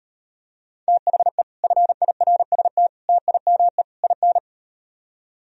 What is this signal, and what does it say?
Morse code, heard as a keyed tone, about 38 wpm, 700 Hz. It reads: THE FIRST TIME IN